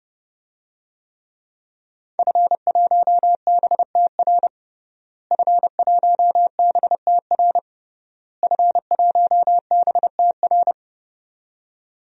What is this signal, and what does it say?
Morse code, heard as a keyed tone, about 30 wpm, 700 Hz. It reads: F1BTR F1BTR F1BTR
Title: DXCC-100 Call Signs - 5 Characters - Repeated 3x - 30wpm